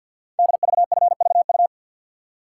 Morse code, heard as a keyed tone, about 49 wpm, 700 Hz. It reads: DVFVU
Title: Code Groups - 5 Letters - 50wpm